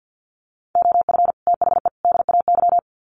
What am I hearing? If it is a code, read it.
Morse code, heard as a keyed tone, about 49 wpm, 700 Hz. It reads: OF THE DAY
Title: Sets of 3 Words 50wpm